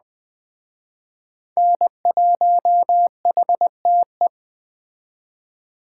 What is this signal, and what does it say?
Morse code, heard as a keyed tone, about 22 wpm, 700 Hz. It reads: N1HTE